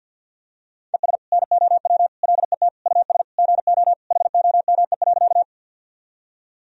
Morse code, heard as a keyed tone, about 45 wpm, 700 Hz. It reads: ES NOW LET US GO HOME.
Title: Sentences from Top 300 Words and Abbreviations 50wpm